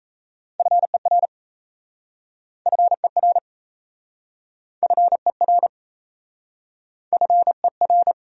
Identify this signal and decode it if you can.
Morse code, heard as a keyed tone, about 38 wpm, 700 Hz. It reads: FER FER FER FER